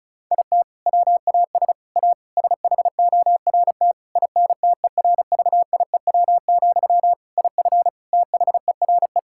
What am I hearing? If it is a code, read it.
Morse code, heard as a keyed tone, about 36 wpm, 700 Hz. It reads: IT WAS A SHORT INTERVIEW, IF THERE
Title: Sentences from Top 1600 Words 35wpm